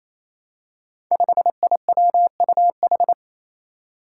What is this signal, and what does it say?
Morse code, heard as a keyed tone, about 28 wpm, 700 Hz. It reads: 5IWUH
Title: Five Challenging Letter-Number Combinations 28wpm